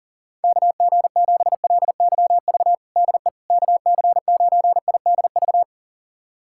Morse code, heard as a keyed tone, about 40 wpm, 700 Hz. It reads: KG7LYV DE KC9IDV
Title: USA Callsign DE Callsign - 6 Characters - 40wpm